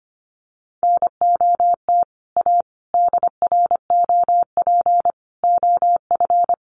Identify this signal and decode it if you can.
Morse code, heard as a keyed tone, about 25 wpm, 700 Hz. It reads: NOT A DROP OF
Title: Sentences from Top 1400 Words 25wpm